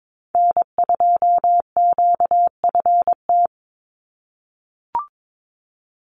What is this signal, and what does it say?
Morse code, heard as a keyed tone, about 22 wpm, 700 Hz. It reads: N2QFT
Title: USA Call Signs 22wpm